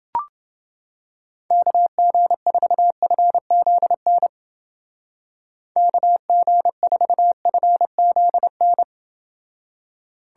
Morse code, heard as a keyed tone, about 30 wpm, 700 Hz. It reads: KG4FZN KG4FZN